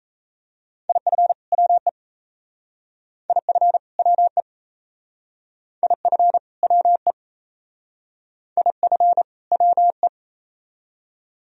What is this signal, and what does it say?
Morse code, heard as a keyed tone, about 34 wpm, 700 Hz. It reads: IF WE IF WE IF WE IF WE
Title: Speed-Racing - Sets of 2 Words 28wpm